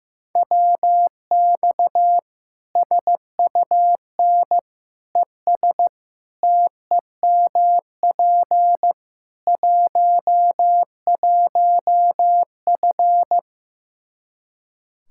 Morse code, heard as a keyed tone, about 16 wpm, 700 Hz. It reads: WX SUN ES TEMP 11F
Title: Practice QSO - Exchange 2 - Info - 15wpm